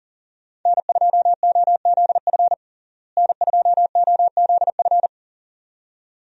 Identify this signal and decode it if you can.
Morse code, heard as a keyed tone, about 40 wpm, 700 Hz. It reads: N2OZF N2OZF
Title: USA Call Signs - 5 Characters - Repeated 2x - 40wpm